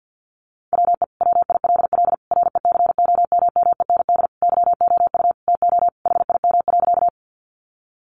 Morse code, heard as a keyed tone, about 50 wpm, 700 Hz. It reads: WE WILL RECOMMEND YOU TO HIM.
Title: Sentences from Top 1200 Words 50wpm